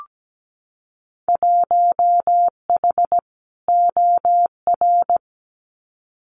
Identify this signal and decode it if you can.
Morse code, heard as a keyed tone, about 17 wpm, 700 Hz. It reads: 1H OR
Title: Winter Field Day Exchanges 17wpm